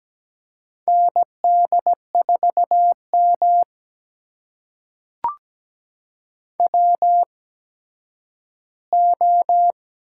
Morse code, heard as a keyed tone, about 17 wpm, spaced as for 10 wpm, 700 Hz. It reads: ND4M WO
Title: USA Call Signs - 4 Characters - 8x Character Spacing - 17wpm